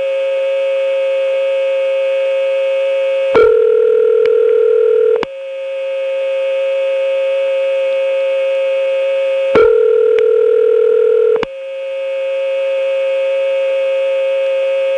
Ringback tone (North America), 440 + 480 Hz.
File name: phonetrips / SmyrnaGA5HQ